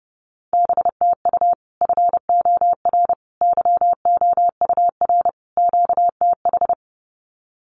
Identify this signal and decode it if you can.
Morse code, heard as a keyed tone, about 30 wpm, 700 Hz. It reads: BTU FOR YOUR QTH
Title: Common QSO Phrases 30wpm